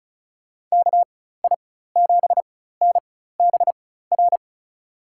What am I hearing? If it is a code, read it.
Morse code, heard as a keyed tone, about 36 wpm, spaced as for 16 wpm, 700 Hz. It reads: KI7NBR
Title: USA Call Signs - 4x Character Spacing - 35wpm